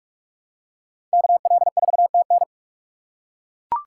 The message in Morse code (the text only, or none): KL4TN